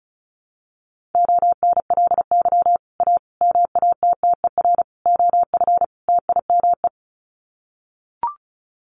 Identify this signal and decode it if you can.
Morse code, heard as a keyed tone, about 34 wpm, 700 Hz. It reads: ONLY A MATTER OF TIME
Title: Sets of 5 Words Encore 35wpm